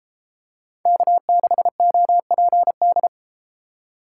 Morse code, heard as a keyed tone, about 32 wpm, 700 Hz. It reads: K6OPD